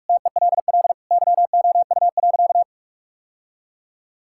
Morse code, heard as a keyed tone, about 46 wpm, 700 Hz. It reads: TELL YOU.
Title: Sentences from Top 1600 Words 45wpm